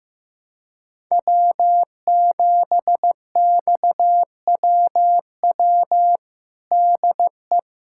Morse code, heard as a keyed tone, about 16 wpm, 700 Hz. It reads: W7XWW DE
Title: USA Callsign DE Callsign - 5 Characters - 15wpm